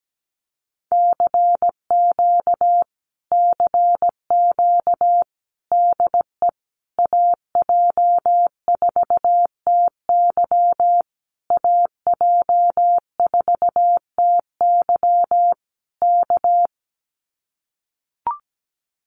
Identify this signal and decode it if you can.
Morse code, heard as a keyed tone, about 17 wpm, 700 Hz. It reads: CQ CQ DE AJ4TY AJ4TY K
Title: Common QSO Phrases Encore 17wpm